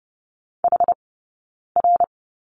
Morse code, heard as a keyed tone, about 30 wpm, spaced as for 9 wpm, 690 Hz. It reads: HR
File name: Lesson 19d - Letter w - Two Characters 30wpm